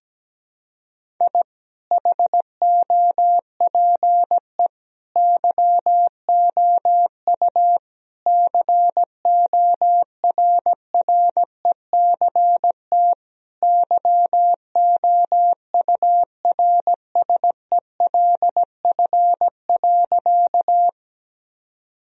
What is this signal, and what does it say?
Morse code, heard as a keyed tone, about 17 wpm, 700 Hz. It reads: I HOPE YOU CORRECT YOURSELF.
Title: Sentences from Top 1000 Words - Review Part 2 - 17wpm